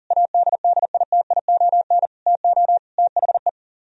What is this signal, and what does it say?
Morse code, heard as a keyed tone, about 51 wpm, 700 Hz. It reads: ADDITION TO THE